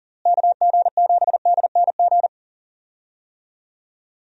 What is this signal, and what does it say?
Morse code, heard as a keyed tone, about 40 wpm, 700 Hz. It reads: KG7DNG
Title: USA Call Signs - 6 Characters - Repeated 2x - 40wpm